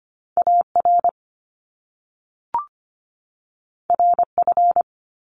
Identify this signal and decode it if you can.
Morse code, heard as a keyed tone, about 25 wpm, 700 Hz. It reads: A R R F
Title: Top Two Letter Combinations in Words 25wpm